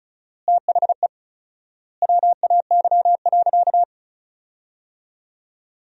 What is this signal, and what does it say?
Morse code, heard as a keyed tone, about 36 wpm, 700 Hz. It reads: THE WAY.
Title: 4x Word-Spacing - Sentences from Top 200 Words - 35wpm